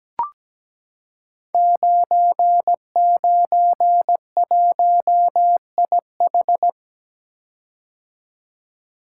Morse code, heard as a keyed tone, about 17 wpm, 700 Hz. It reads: 991IH
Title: Code Groups - 5 Letter:Numbers - 17wpm